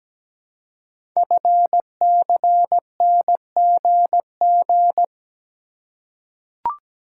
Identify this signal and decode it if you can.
Morse code, heard as a keyed tone, about 17 wpm, 700 Hz. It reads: FCNGG
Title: Code Groups - 5 Letters - 17wpm